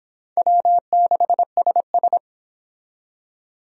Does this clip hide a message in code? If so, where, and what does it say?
Yes: Morse code from 0.4 to 2.2 s, heard as a keyed tone, about 26 wpm, 700 Hz. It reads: W6SS